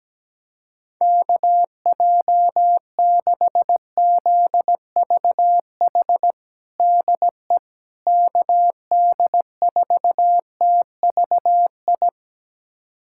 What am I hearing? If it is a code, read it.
Morse code, heard as a keyed tone, about 17 wpm, 700 Hz. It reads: KJ6ZVH DE KD4TVI
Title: USA Callsign DE Callsign 17wpm